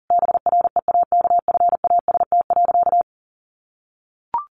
Morse code, heard as a keyed tone, about 40 wpm, 700 Hz. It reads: BREAKFAST.
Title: Sentences from Top 1500 Words 40wpm